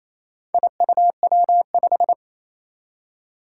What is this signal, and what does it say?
Morse code, heard as a keyed tone, about 28 wpm, 700 Hz. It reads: IUW5
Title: Four Challenging Letter-Number Combinations 28wpm